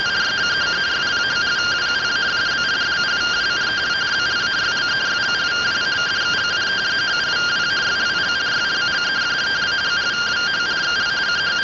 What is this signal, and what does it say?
Two-tone teleprinter signal, about 170 Hz shift.